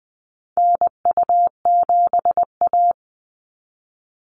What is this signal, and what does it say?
Morse code, heard as a keyed tone, about 20 wpm, 700 Hz. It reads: NU7A
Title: NA SOTA Activator Callsigns 20wpm